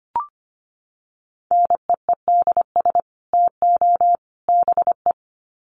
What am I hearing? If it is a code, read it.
Morse code, heard as a keyed tone, about 25 wpm, 700 Hz. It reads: NEEDS TO BE
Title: Sets of 3 Words 25wpm